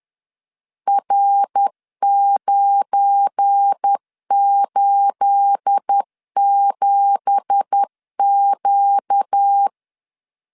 Morse code, heard as a keyed tone, about 10 wpm, 800 Hz. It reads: R987Q
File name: LEZIONI / 9 2867 10wpm 2733